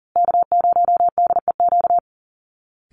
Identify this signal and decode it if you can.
Morse code, heard as a keyed tone, about 40 wpm, 700 Hz. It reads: K0DEQ
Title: Call Signs Repeated 2x - 40wpm